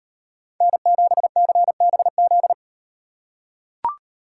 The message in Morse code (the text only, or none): N7CBZ